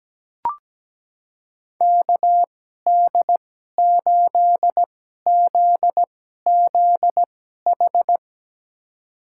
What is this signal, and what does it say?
Morse code, heard as a keyed tone, about 17 wpm, spaced as for 12 wpm, 700 Hz. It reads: KD8ZZH
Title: USA Call Signs - 6 Characters - 2x Character Spacing - 17wpm